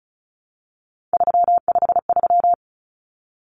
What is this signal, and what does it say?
Morse code, heard as a keyed tone, about 34 wpm, 700 Hz. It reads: 353
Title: Numbers - 3-Digit 35wpm